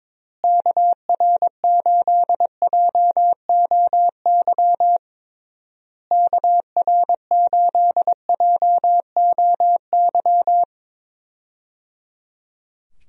Morse code, heard as a keyed tone, about 22 wpm, 700 Hz. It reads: KR8JOY KR8JOY
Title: USA Call Signs - 6 Characters - Repeated 3x - 22wpm